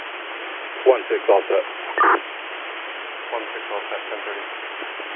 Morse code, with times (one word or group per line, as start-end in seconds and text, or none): none